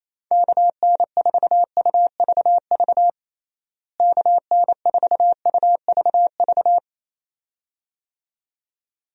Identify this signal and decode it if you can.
Morse code, heard as a keyed tone, about 28 wpm, 700 Hz. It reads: KN4UVV KN4UVV
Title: USA Call Signs - 6 Characters - Repeated 2x - 28wpm